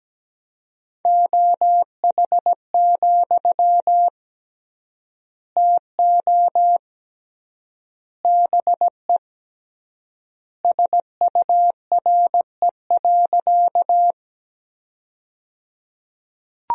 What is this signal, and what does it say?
Morse code, heard as a keyed tone, about 17 wpm, 700 Hz. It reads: OH, TO BE SURE.
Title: 3x Word-Spacing - Sentences from Top 500 Words - 17wpm